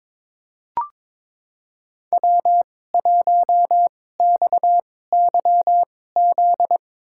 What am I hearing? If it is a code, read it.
Morse code, heard as a keyed tone, about 22 wpm, spaced as for 16 wpm, 700 Hz. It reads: W1XYZ
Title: USA Call Signs - 5 Characters - 2x Character Spacing - 22wpm